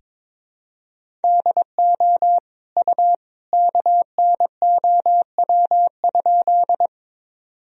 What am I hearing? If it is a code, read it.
Morse code, heard as a keyed tone, about 22 wpm, 700 Hz. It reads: DO U KNOW?
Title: Sentences from Top 100 Words and Abbreviations 22wpm